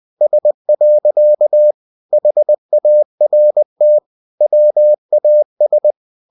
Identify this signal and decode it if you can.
Morse code, heard as a keyed tone, about 20 wpm, 600 Hz. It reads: S. HART WAS
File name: cw-books / winnie-the-pooh / winnie-20wpm-page-0122